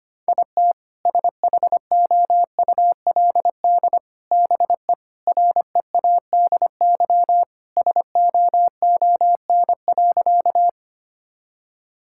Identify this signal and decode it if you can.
Morse code, heard as a keyed tone, about 25 wpm, 700 Hz. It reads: IT SHOULD BE READY SOON.
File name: practice / 25wpm / Sentences with 2-Word Phrases - 25wpm